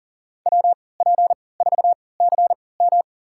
Morse code, heard as a keyed tone, about 40 wpm, spaced as for 23 wpm, 700 Hz. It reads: WP4CM
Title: USA Call Signs - 5 Characters - 3x Character Spacing - 40wpm